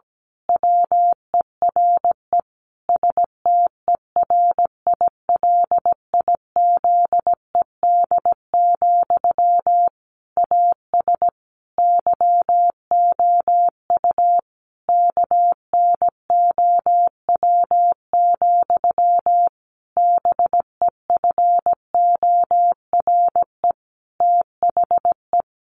Morse code, heard as a keyed tone, about 17 wpm, 700 Hz. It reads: WERE STERILIZED, AS YOU KNOW, BEFORE THE
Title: A Sound Of Thunder 17wpm